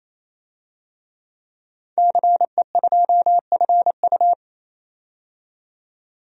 Morse code, heard as a keyed tone, about 27 wpm, 700 Hz. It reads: CE2FU
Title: DXCC-100 Call Signs - 5 Characters - 28wpm